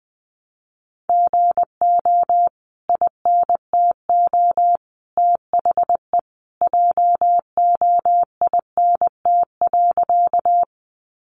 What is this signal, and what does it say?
Morse code, heard as a keyed tone, about 20 wpm, 700 Hz. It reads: GO INTO THE JOINT.
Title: Sentences from Top 1300 Words 20wpm